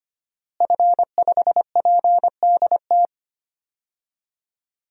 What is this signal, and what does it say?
Morse code, heard as a keyed tone, about 25 wpm, 700 Hz. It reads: F5PDT
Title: DXCC-100 Call Signs - 5 Characters - Repeated 2x - 25wpm